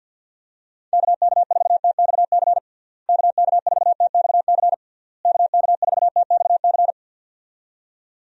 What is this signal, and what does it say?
Morse code, heard as a keyed tone, about 46 wpm, 700 Hz. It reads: KK4TXC KK4TXC KK4TXC
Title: USA Call Signs - 6 Characters - Repeated 3x - 50wpm